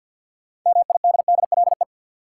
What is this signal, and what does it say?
Morse code, heard as a keyed tone, about 53 wpm, 700 Hz. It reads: MIDDLE
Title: Sentences from Top 900 Words - New Word Drill - 50wpm